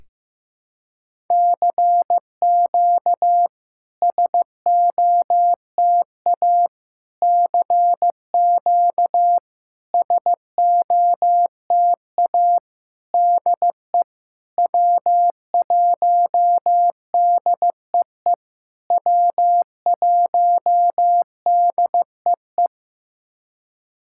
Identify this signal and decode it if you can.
Morse code, heard as a keyed tone, about 15 wpm, 700 Hz. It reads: CQ SOTA CQ SOTA DE W1DEE W1DEE
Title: SOTA QSO Practice 15wpm